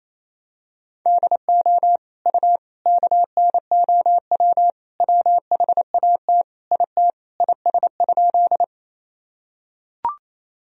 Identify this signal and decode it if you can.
Morse code, heard as a keyed tone, about 28 wpm, 700 Hz. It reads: DO U KNOW WHAT IT IS?
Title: Sentences from Top 100 Words and Abbreviations 28wpm